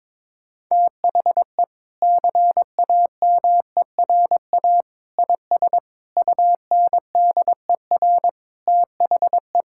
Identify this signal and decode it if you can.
Morse code, heard as a keyed tone, about 22 wpm, 700 Hz. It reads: THE CAMERA IS UNDER THE
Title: Sentences from Top 1000 Words - Review Part 5 - 22wpm